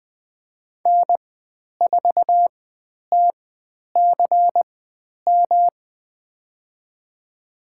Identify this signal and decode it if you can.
Morse code, heard as a keyed tone, about 20 wpm, spaced as for 10 wpm, 700 Hz. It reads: N4TCM
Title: 109 Lesson 18g - Letter m - Call Signs 20wpm